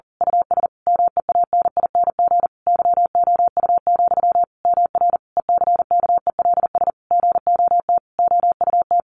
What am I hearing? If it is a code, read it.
Morse code, heard as a keyed tone, about 40 wpm, 700 Hz. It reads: US MEANING YOU, MR ECKELS GOT OUT
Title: A Sound Of Thunder 40wpm